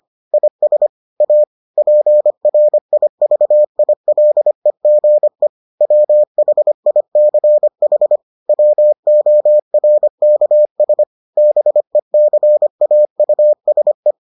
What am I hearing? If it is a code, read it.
Morse code, heard as a keyed tone, about 25 wpm, 600 Hz. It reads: IS A PRIVILEGE WHICH WORKS BECAUSE